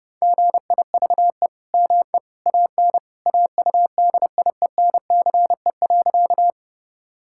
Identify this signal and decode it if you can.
Morse code, heard as a keyed tone, about 30 wpm, 700 Hz. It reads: GIVE ME AN AUDIENCE.
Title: Sentences from Top 1000 Words - Review Part 5 - 30wpm